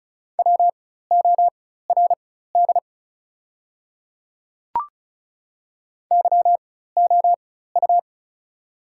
Morse code, heard as a keyed tone, about 36 wpm, spaced as for 16 wpm, 690 Hz. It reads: WORD YOU
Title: 4x Character Spacing - Top 100 Words - 35wpm